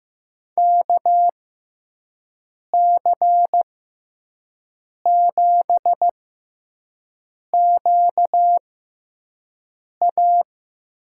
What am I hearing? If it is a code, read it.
Morse code, heard as a keyed tone, about 15 wpm, spaced as for 5 wpm, 700 Hz. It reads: KC7QA
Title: USA Call Signs - 6x Character Spacing - 15wpm